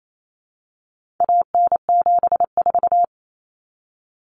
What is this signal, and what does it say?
Morse code, heard as a keyed tone, about 28 wpm, 700 Hz. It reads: AN74
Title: Common Cut Numbers - 4-Digit 28wpm